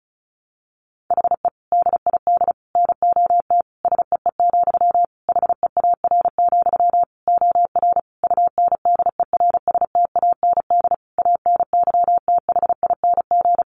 Morse code, heard as a keyed tone, about 34 wpm, 700 Hz. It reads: HE DID NOT SEE, HEAR, OR UNDERSTAND ANYTHING
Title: Sentences from Top 1300 Words 35wpm